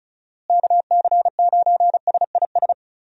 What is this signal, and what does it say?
Morse code, heard as a keyed tone, about 36 wpm, 700 Hz. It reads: KC9SIS
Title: USA Call Signs - 6 Characters - Repeated 2x - 35wpm